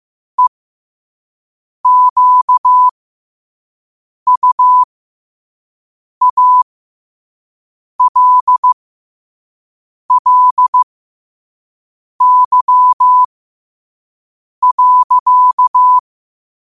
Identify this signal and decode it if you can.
Morse code, heard as a keyed tone, about 15 wpm, spaced as for 5 wpm, 1000 Hz. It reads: EQUALLY.
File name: qotdfiles / 5wpm